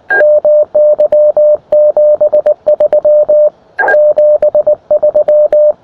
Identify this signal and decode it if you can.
Morse code, heard as a keyed tone, about 19 wpm, 620 Hz. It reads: MY 73 73
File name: attachments / uy1if 20m